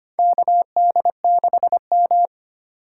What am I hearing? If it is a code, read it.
Morse code, heard as a keyed tone, about 25 wpm, 700 Hz. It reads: KD6M
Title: USA Call Signs - 4 Characters - 2x Character Spacing - 25wpm